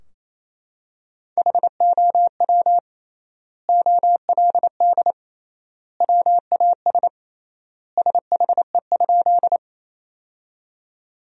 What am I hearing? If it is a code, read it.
Morse code, heard as a keyed tone, about 28 wpm, 700 Hz. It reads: HOW OLD WAS SHE?
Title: 3x Word-Spacing - Sentences from Top 200 Words - 28wpm